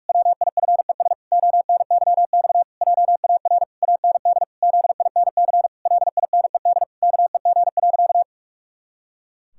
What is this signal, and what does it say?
Morse code, heard as a keyed tone, about 46 wpm, 700 Hz. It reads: WIFES ONYX JAR AND ZINC LINED KEG.
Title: Pangram Challenge v2 - 45wpm